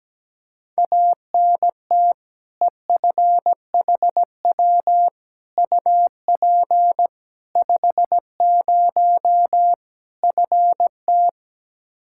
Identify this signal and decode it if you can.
Morse code, heard as a keyed tone, about 17 wpm, 700 Hz. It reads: ANT EFHW UP 50 FT
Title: Practice QSO - Exchange 2 - Info - 17wpm